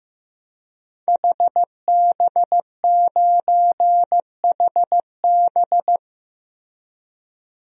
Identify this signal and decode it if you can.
Morse code, heard as a keyed tone, about 15 wpm, 700 Hz. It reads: HB9HB
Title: DXCC-100 Call Signs - 5 Characters - Repeated 2x - 15wpm